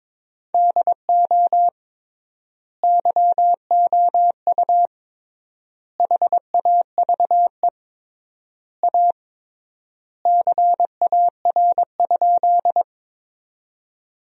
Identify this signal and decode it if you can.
Morse code, heard as a keyed tone, about 22 wpm, 700 Hz. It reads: DO YOU HAVE A CAR?
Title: 3x Word-Spacing - Sentences from Top 300 Words - 22wpm